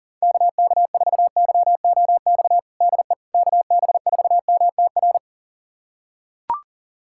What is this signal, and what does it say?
Morse code, heard as a keyed tone, about 37 wpm, 700 Hz. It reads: KK4YOX DE KB4MTR
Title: USA Callsign DE Callsign 40wpm